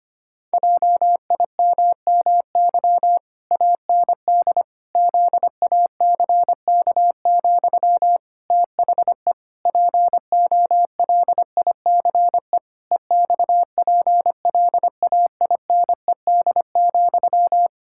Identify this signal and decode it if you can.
Morse code, heard as a keyed tone, about 25 wpm, 700 Hz. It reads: JIMMY AND ZACK, THE POLICE EXPLAINED,
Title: Pangram Challenge v2 - 25wpm